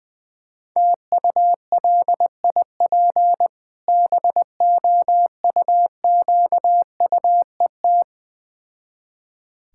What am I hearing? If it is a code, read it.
Morse code, heard as a keyed tone, about 20 wpm, 690 Hz. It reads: TULIP BOUQUET
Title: Difficult Words with Hints - 20wpm